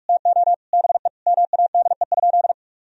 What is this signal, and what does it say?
Morse code, heard as a keyed tone, about 45 wpm, 700 Hz. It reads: TO BE MADE?
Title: Sentences from Top 1000 Words - Review Part 4 - 45wpm